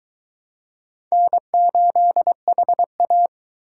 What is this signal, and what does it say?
Morse code, heard as a keyed tone, about 23 wpm, 700 Hz. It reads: N8HA